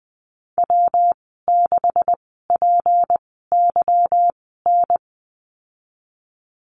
Morse code, heard as a keyed tone, about 20 wpm, spaced as for 15 wpm, 700 Hz. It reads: W6PYN
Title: 186 Lesson 30g - Letter b - Call Signs 20wpm